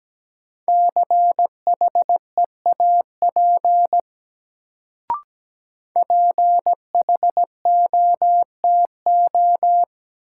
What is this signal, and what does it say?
Morse code, heard as a keyed tone, about 17 wpm, 700 Hz. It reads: CHEAP PHOTO
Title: Sentences from Top 1200 Words - New Word Drill - 17wpm